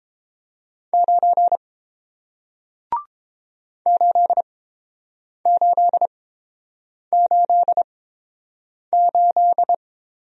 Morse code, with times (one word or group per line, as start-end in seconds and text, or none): 0.9-1.6 9
3.9-9.7 8888